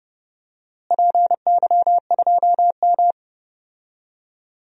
Morse code, heard as a keyed tone, about 31 wpm, 700 Hz. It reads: PY2M
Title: DXCC-100 Call Signs - 4 Characters - Repeated 3x - 30wpm